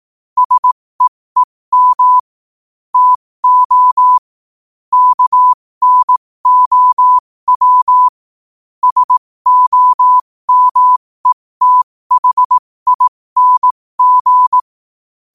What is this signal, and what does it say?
Morse code, heard as a keyed tone, about 18 wpm, spaced as for 15 wpm, 1000 Hz. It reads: SEEM TO KNOW SOMETHING
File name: qotdfiles / 15wpm